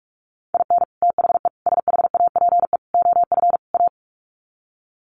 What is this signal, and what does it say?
Morse code, heard as a keyed tone, about 44 wpm, 700 Hz. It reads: IN THE SHAPE OF A
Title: Sets of 5 Words Encore 45wpm